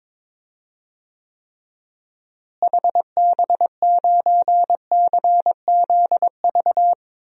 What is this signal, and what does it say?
Morse code, heard as a keyed tone, about 21 wpm, 700 Hz. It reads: HB9CZV
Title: DXCC-100 Call Signs - 6 Characters - Repeated 3x - 22wpm